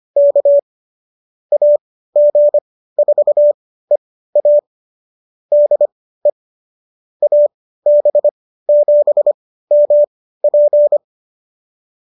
Morse code, heard as a keyed tone, about 25 wpm, spaced as for 15 wpm, 590 Hz. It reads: K AG4EA DE AB7MP